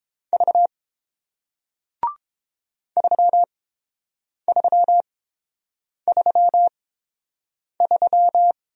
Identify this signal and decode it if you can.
Morse code, heard as a keyed tone, about 31 wpm, spaced as for 7 wpm, 700 Hz. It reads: V 3333